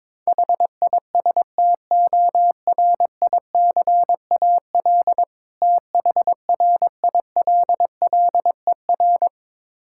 Morse code, heard as a keyed tone, about 22 wpm, 690 Hz. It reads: HISTORICAL THRILLER
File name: Difficult Words with Hints - 22wpm